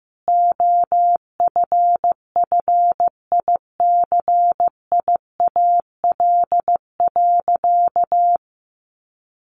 Morse code, heard as a keyed tone, about 15 wpm, 700 Hz. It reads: OFFICIAL.
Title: Sentences from Top 1000 Words 15wpm